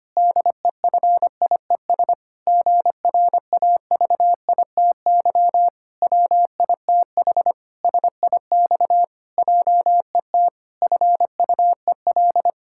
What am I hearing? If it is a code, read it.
Morse code, heard as a keyed tone, about 25 wpm, 700 Hz. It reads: DEFIES GRAVITY WITH SIX JET FUEL
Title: Pangram Challenge v2 - 25wpm